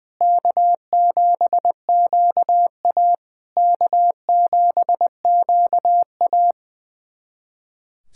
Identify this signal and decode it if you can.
Morse code, heard as a keyed tone, about 20 wpm, 700 Hz. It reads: K7QA K7QA
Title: SOTA QSO Practice 20wpm